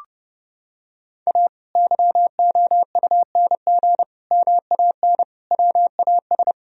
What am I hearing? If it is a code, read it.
Morse code, heard as a keyed tone, about 30 wpm, 700 Hz. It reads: A YOUNG MAN WAS
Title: Sentences from Top 1000 Words - Review Part 3 - 30wpm